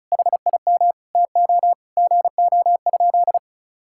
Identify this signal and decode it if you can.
Morse code, heard as a keyed tone, about 34 wpm, 700 Hz. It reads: HIM TO GO?
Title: Sentences from Top 1600 Words 35wpm